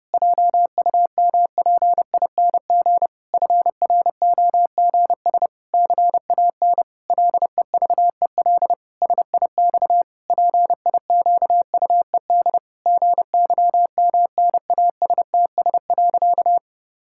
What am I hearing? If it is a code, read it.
Morse code, heard as a keyed tone, about 29 wpm, 700 Hz. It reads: JUMPING FROGS CAN LEVEL SIX PIQUED GYMNASTS.
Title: Pangram Challenge v2 - 30wpm